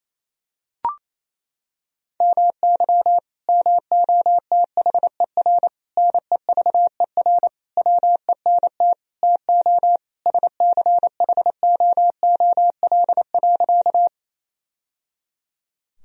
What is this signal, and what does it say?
Morse code, heard as a keyed tone, about 28 wpm, 700 Hz. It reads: MY MOTHER NEVER WENT TO SCHOOL.
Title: Sentences from Top 1000 Words - Review Part 1 - 28wpm